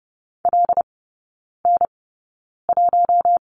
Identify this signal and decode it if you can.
Morse code, heard as a keyed tone, about 30 wpm, spaced as for 9 wpm, 700 Hz. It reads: LN1
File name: course / 30:10wpm / 087 Lesson 15e - Number 5 - Three Characters 30wpm